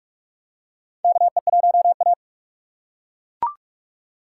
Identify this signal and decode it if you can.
Morse code, heard as a keyed tone, about 41 wpm, 700 Hz. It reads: KE1A
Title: USA Call Signs - 4 Characters - 2x Character Spacing - 45wpm